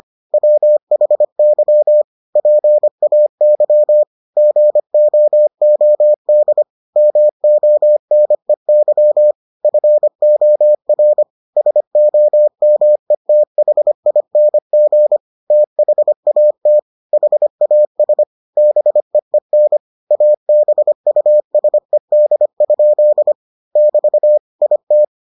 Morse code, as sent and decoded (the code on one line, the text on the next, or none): .-- .... -.-- / .--. .- -.-- / --. --- --- -.. / -- --- -. -.-- / ..-. --- .-. / ... --- -- . - .... .. -. --. / - .... .- - / .... .- ... / -... . -. / .- -... ..- ... -.. ..--.. / -...- / .. -
WHY PAY GOOD MONEY FOR SOMETHING THAT HAS BEEN ABUSED? <BT> IT